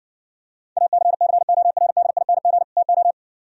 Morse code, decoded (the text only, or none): ACCORDING TO